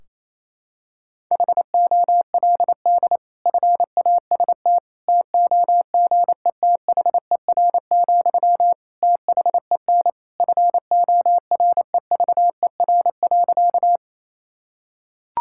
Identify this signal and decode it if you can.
Morse code, heard as a keyed tone, about 28 wpm, 700 Hz. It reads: HOLD FAST TOGETHER, THEN FOREVER.
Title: Sentences from Top 1600 Words 28wpm